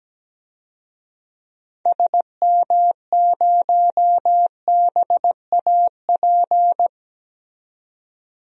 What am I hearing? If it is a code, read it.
Morse code, heard as a keyed tone, about 17 wpm, 700 Hz. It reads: SM0BAP
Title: DXCC-100 Call Signs - 6 Characters - 17wpm